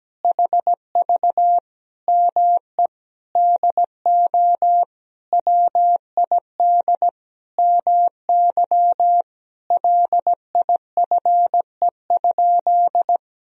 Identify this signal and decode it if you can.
Morse code, heard as a keyed tone, about 17 wpm, 700 Hz. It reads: HV ME DO WID MY LIFE?
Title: Sentences from Top 300 Words and Abbreviations 17wpm